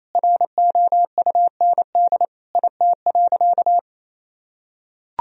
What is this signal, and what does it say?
Morse code, heard as a keyed tone, about 28 wpm, 700 Hz. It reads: ROUND IT.
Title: Sentences from Top 1000 Words - Review Part 1 - 28wpm